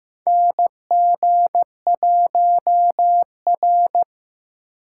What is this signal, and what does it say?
Morse code, heard as a keyed tone, about 15 wpm, 700 Hz. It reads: NG1R
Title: USA Call Signs - 4 Characters - 6x Character Spacing - 15wpm